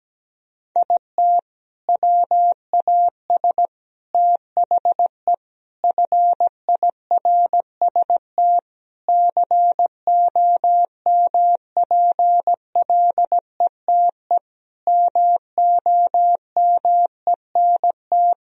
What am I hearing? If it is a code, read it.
Morse code, heard as a keyed tone, about 17 wpm, 700 Hz. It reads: IT WAS THE FIRST COMPLETE MOMENT
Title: Sentences from Top 1000 Words - Review Part 2 - 17wpm